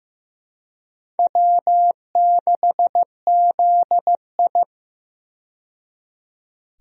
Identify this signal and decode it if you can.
Morse code, heard as a keyed tone, about 15 wpm, 700 Hz. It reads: W6ZI
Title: USA Call Signs - 4 Characters - 15wpm